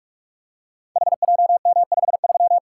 Morse code, heard as a keyed tone, about 44 wpm, 700 Hz. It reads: HJM53
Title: Code Groups - 5 Letter:Numbers - 45wpm